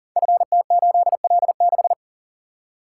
Morse code, heard as a keyed tone, about 42 wpm, 700 Hz. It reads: FT8L6
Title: Code Groups - 5 Letter:Numbers - 40wpm